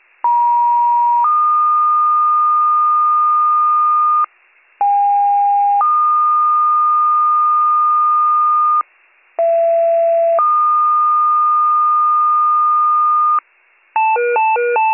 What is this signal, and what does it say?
Two-tone page, heard as a one-second tone, then a three-second tone, about 950 then 1220 Hz, about 810 then 1220 Hz, about 670 then 1220 Hz. It is heard from 0.2 to 13.4 s.